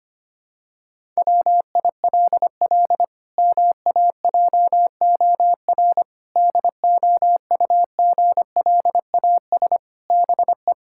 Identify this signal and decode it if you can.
Morse code, heard as a keyed tone, about 25 wpm, 700 Hz. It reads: WILL MAJOR DOUGLAS BE
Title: Pangram Challenge v2 - 25wpm